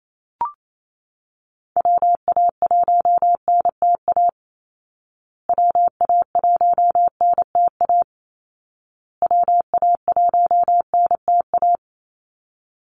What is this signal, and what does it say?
Morse code, heard as a keyed tone, about 28 wpm, 700 Hz. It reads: WA1NTA WA1NTA WA1NTA
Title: Call Signs Repeated 2x - 28wpm